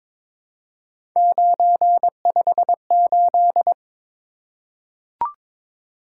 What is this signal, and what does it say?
Morse code, heard as a keyed tone, about 22 wpm, 700 Hz. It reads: 958
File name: practice / 22wpm / Three Letter-Number Combinations - 22wpm